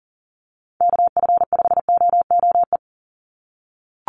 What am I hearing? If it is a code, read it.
Morse code, heard as a keyed tone, about 40 wpm, 700 Hz. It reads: KF5OOE
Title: USA Call Signs 40wpm